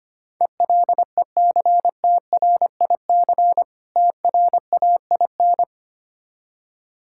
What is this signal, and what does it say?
Morse code, heard as a keyed tone, about 25 wpm, 690 Hz. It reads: ELECTRIC TRAIN
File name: practice / 25wpm / Difficult Words with Hints - 25wpm